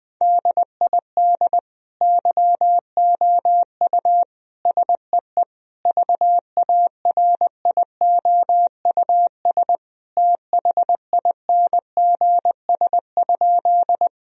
Morse code, heard as a keyed tone, about 20 wpm, 700 Hz. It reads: DID YOU SEE VARIOUS THINGS?
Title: Sentences from Top 1000 Words - Review Part 4 - 20wpm